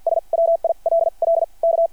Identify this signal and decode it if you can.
Morse code, heard as a keyed tone, about 53 wpm, spaced as for 41 wpm, 670 Hz. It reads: S WILL K
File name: archived / CRSnet